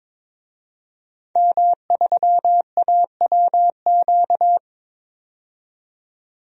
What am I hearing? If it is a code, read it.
Morse code, heard as a keyed tone, about 22 wpm, 700 Hz. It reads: M3AWQ
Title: DXCC-100 Call Signs - 5 Characters - 22wpm